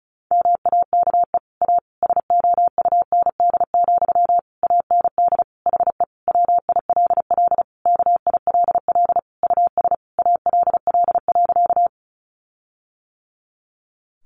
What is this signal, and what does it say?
Morse code, heard as a keyed tone, about 34 wpm, 700 Hz. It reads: MAKE A SOUND, AND HE WILL KILL US ALL.
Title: Sentences from Top 1400 Words 35wpm